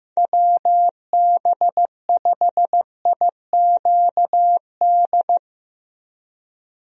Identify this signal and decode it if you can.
Morse code, heard as a keyed tone, about 15 wpm, 700 Hz. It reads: WB5IQD
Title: USA Call Signs - 6 Characters - 15wpm